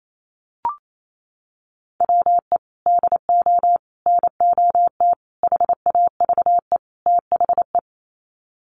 Morse code, heard as a keyed tone, about 28 wpm, 700 Hz. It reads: WE DO NOT HAVE THE
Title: Sets of 5 Words 28wpm